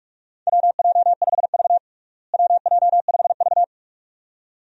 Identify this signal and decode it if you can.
Morse code, heard as a keyed tone, about 46 wpm, 700 Hz. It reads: WJ5V WJ5V
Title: USA Call Signs - 4 Characters - Repeated 2x - 45wpm